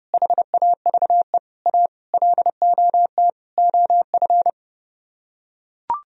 Morse code, heard as a keyed tone, about 32 wpm, 700 Hz. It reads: HAVE A LOT OF